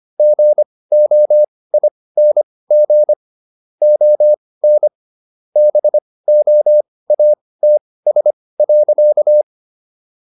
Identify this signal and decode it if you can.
Morse code, heard as a keyed tone, about 25 wpm, spaced as for 18 wpm, 600 Hz. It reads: GOING ON BOATS.